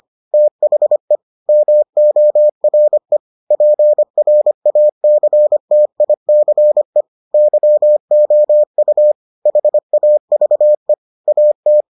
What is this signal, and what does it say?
Morse code, heard as a keyed tone, about 25 wpm, 600 Hz. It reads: THE MORE PRACTICE YOU HAVE AT